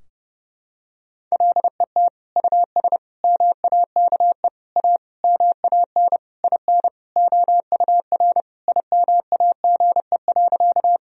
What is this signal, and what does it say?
Morse code, heard as a keyed tone, about 30 wpm, 700 Hz. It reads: LET US MAKE A MAN IN OUR IMAGE.
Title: Sentences from Top 1000 Words - Review Part 4 - 30wpm